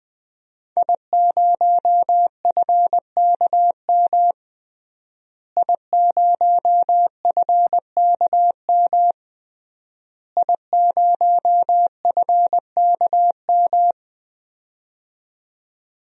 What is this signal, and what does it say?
Morse code, heard as a keyed tone, about 20 wpm, 700 Hz. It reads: I0FKM I0FKM I0FKM
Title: DXCC-100 Call Signs - 5 Characters - Repeated 3x - 20wpm